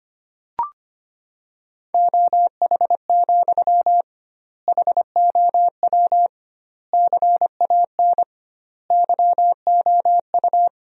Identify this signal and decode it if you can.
Morse code, heard as a keyed tone, about 25 wpm, 700 Hz. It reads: OH, HOW CAN YOU
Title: 2x Word-Spacing - Sentences from Top 500 Words - 25wpm